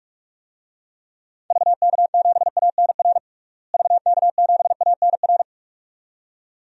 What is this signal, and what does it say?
Morse code, heard as a keyed tone, about 43 wpm, 700 Hz. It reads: VK7ANR VK7ANR
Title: DXCC-100 Call Signs - 6 Characters - Repeated 2x - 45wpm